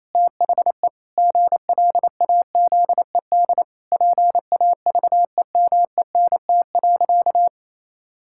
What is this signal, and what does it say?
Morse code, heard as a keyed tone, about 27 wpm, 700 Hz. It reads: THE GLAZED PAVEMENT.
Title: Pangram Challenge v2 - 28wpm